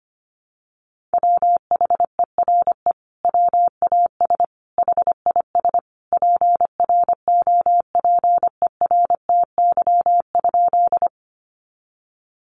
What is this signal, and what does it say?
Morse code, heard as a keyed tone, about 25 wpm, 700 Hz. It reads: WHERE WAS HIS PROPERTY?
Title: Sentences from Top 1000 Words 25wpm